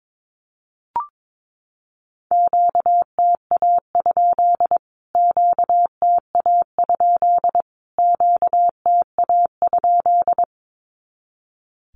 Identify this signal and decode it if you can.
Morse code, heard as a keyed tone, about 22 wpm, 700 Hz. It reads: QTA? QTA? QTA?
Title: Q-Codes Repeated 3x 22wpm